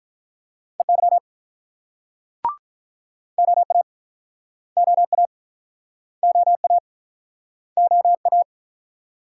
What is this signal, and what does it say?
Morse code, heard as a keyed tone, about 46 wpm, 700 Hz. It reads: EY OA OA OA OA